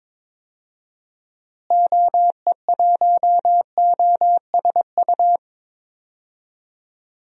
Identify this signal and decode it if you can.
Morse code, heard as a keyed tone, about 22 wpm, 700 Hz. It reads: OE1OSU
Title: DXCC-100 Call Signs - 6 Characters - Repeated 2x - 22wpm